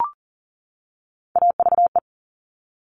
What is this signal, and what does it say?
Morse code, heard as a keyed tone, about 40 wpm, 700 Hz. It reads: AVE
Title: Top Three Letter Combinations in Words 40wpm